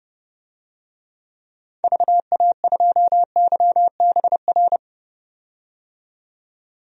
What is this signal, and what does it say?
Morse code, heard as a keyed tone, about 30 wpm, 700 Hz. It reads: VA2YBR
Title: DXCC-100 Call Signs - 6 Characters - 30wpm